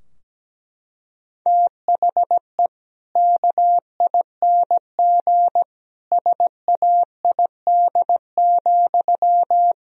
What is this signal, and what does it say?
Morse code, heard as a keyed tone, about 17 wpm, 700 Hz. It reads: THE KING SAID,
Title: Sentences from Top 1000 Words - Review Part 3 - 17wpm